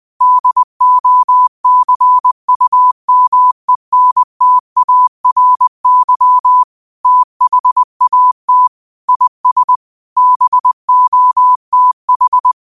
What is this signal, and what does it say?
Morse code, heard as a keyed tone, about 20 wpm, 1000 Hz. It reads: DOCUMENTARY THAT IS BOTH
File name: qotdfiles / 20wpm